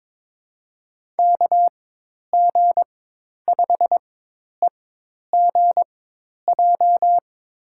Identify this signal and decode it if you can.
Morse code, heard as a keyed tone, about 22 wpm, spaced as for 10 wpm, 700 Hz. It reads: KG5EGJ
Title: USA Call Signs - 6 Characters - 4x Character Spacing - 22wpm